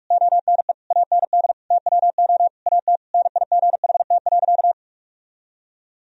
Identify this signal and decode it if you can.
Morse code, heard as a keyed tone, about 41 wpm, 700 Hz. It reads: ONE AND TWO AT NIGHT.
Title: Sentences from Top 1000 Words - Review Part 1 - 45wpm